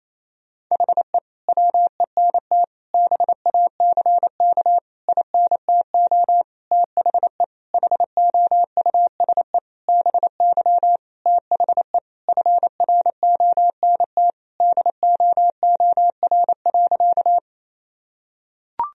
Morse code, heard as a keyed tone, about 28 wpm, 700 Hz. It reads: HE WENT BACK INTO THE HOUSE BY THE FRONT DOOR.
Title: Sentences from Top 1000 Words - Review Part 2 - 28wpm